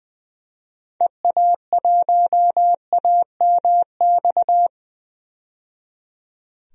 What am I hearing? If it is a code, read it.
Morse code, heard as a keyed tone, about 20 wpm, 700 Hz. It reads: EA1AMX
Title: DXCC-100 Call Signs - 6 Characters - 20wpm